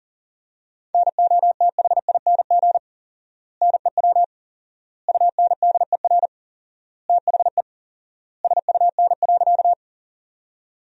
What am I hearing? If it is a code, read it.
Morse code, heard as a keyed tone, about 40 wpm, 700 Hz. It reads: NOTHING NEW UNDER THE SUN.
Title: 4x Word-Spacing - Sentences from Top 500 Words - 40wpm